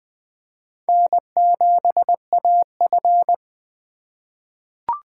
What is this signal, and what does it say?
Morse code, heard as a keyed tone, about 20 wpm, 700 Hz. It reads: N7AF
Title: USA Call Signs - 4 Characters - 3x Character Spacing - 20wpm